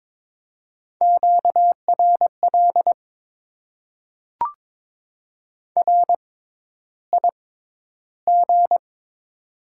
Morse code, heard as a keyed tone, about 22 wpm, spaced as for 11 wpm, 700 Hz. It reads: QRL RIG
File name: Top 20 QSO Elements - 6x Character Spacing - 22wpm